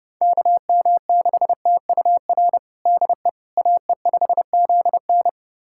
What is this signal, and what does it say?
Morse code, heard as a keyed tone, about 30 wpm, 700 Hz. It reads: KM6TUR DE AE5ZN
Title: USA Callsign DE Callsign 30wpm